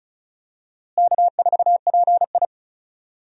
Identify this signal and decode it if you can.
Morse code, heard as a keyed tone, about 36 wpm, 700 Hz. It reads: K4PI
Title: USA Call Signs - 4 Characters - Repeated 2x - 35wpm